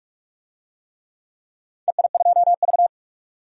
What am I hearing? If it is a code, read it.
Morse code, heard as a keyed tone, about 41 wpm, 700 Hz. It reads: EI2V
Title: DXCC-100 Call Signs - 4 Characters - 45wpm